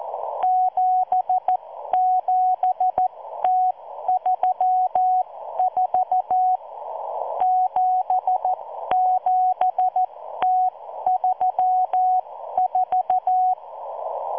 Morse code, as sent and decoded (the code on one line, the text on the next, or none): --... --... - ...-- ....- / --... --... - ...-- ....-
77T34 77T34